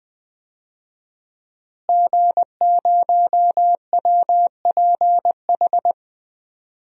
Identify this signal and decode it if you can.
Morse code, heard as a keyed tone, about 21 wpm, 700 Hz. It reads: G0WPH
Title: DXCC-100 Call Signs - 5 Characters - Repeated 3x - 20wpm